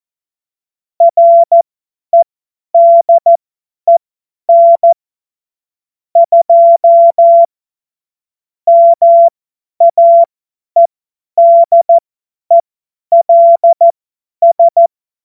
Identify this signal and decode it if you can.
Morse code, heard as a keyed tone, about 14 wpm, spaced as for 10 wpm, 680 Hz. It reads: REDEN 2 MAEDELS
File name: Morsekurs / Witze / Witz 30. November 2021, 50 F70